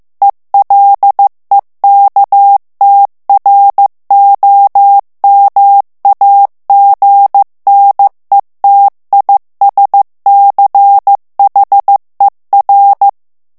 Morse code, heard as a keyed tone, about 15 wpm, 800 Hz. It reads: ELEKTROMAGNETISCHER
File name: morse mp3 / ALLEMAND texte clair 1 vitesse 14 mots minute